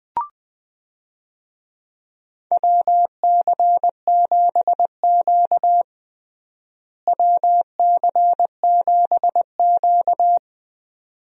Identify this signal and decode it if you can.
Morse code, heard as a keyed tone, about 20 wpm, 700 Hz. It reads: WC7Q WC7Q
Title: USA Call Signs - 4 Characters - Repeated 2x - 20wpm